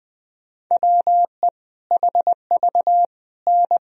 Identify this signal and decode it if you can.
Morse code, heard as a keyed tone, about 20 wpm, 700 Hz. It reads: WE HV N
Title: Sentences from Top 100 Words and Abbreviations 20wpm